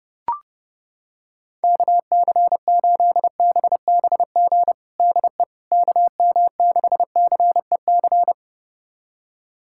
Morse code, heard as a keyed tone, about 32 wpm, 700 Hz. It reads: KC8BBG DE KM6CEC
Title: USA Callsign DE Callsign 30wpm